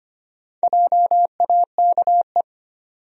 Morse code, heard as a keyed tone, about 25 wpm, 700 Hz. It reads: JAKE
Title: USA Common Names - 25wpm